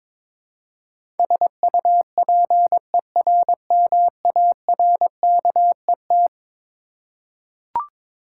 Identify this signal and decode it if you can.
Morse code, heard as a keyed tone, about 22 wpm, 700 Hz. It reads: SUPERMARKET
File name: practice / 22wpm / Eleven Letter Words - 22wpm